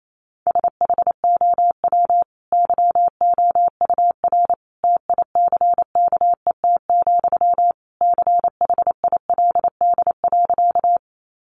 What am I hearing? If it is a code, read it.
Morse code, heard as a keyed tone, about 28 wpm, 700 Hz. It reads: SHOW YOUR TICKET, CHILD.
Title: Sentences from Top 1400 Words 28wpm